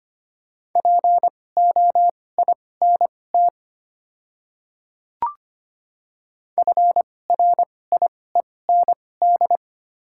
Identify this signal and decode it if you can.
Morse code, heard as a keyed tone, about 25 wpm, spaced as for 18 wpm, 700 Hz. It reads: POINT FRIEND
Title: 2x Character Spacing - Top 300 Words - 25wpm